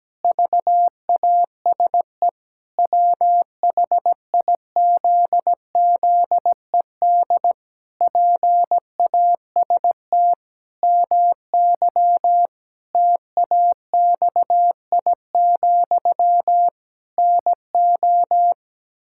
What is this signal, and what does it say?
Morse code, heard as a keyed tone, about 17 wpm, 700 Hz. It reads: VASE WHIZZED PAST MY TAXI, NO
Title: Pangram Challenge v2 - 17wpm